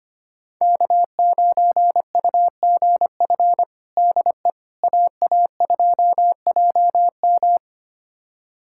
Morse code, heard as a keyed tone, about 25 wpm, 700 Hz. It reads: K9UGF DE AA2JM
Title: USA Callsign DE Callsign - 5 Characters - 25wpm